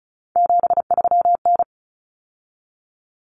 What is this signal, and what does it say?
Morse code, heard as a keyed tone, about 34 wpm, 700 Hz. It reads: 73N
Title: Common Cut Numbers - 3-Digit 35wpm